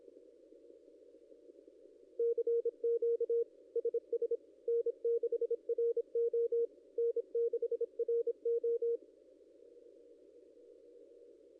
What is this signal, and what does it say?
Morse code, heard as a keyed tone, about 25 wpm, 470 Hz. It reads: CQ SS N6RO N6RO
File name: contests / ss-cw-k5tr-op-n5rz -0900